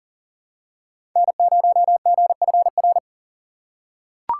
Morse code, heard as a keyed tone, about 40 wpm, 700 Hz. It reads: N0GFR
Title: USA Call Signs - 5 Characters - 6x Character Spacing - 40wpm